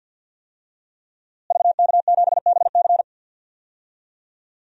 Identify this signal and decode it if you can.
Morse code, heard as a keyed tone, about 52 wpm, 700 Hz. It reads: VK7BC
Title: DXCC-100 Call Signs - 5 Characters - Repeated 3x - 50wpm